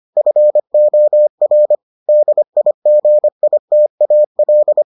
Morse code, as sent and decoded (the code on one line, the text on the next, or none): ..-. --- .-. / -.. .. --. .. - .- .-..
FOR DIGITAL